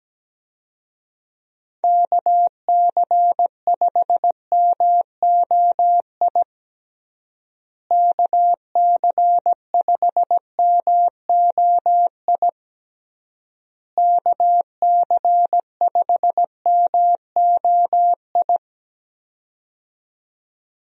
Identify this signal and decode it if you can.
Morse code, heard as a keyed tone, about 17 wpm, 700 Hz. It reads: KC5MOI KC5MOI KC5MOI
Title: USA Call Signs - 6 Characters - Repeated 3x - 17wpm